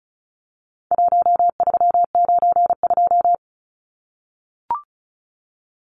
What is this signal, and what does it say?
Morse code, heard as a keyed tone, about 34 wpm, 700 Hz. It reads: 1392
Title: Numbers - Contest Style - 4-Digit 35wpm